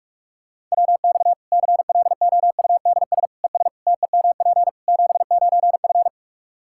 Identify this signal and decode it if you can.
Morse code, heard as a keyed tone, about 41 wpm, 700 Hz. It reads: WX CLOUDS ES TEMP 79F
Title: Practice QSO - Exchange 2 - Info - 45wpm